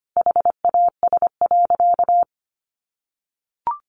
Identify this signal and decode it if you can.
Morse code, heard as a keyed tone, about 25 wpm, 700 Hz. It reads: HAS.
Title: Sentences from Top 1500 Words 25wpm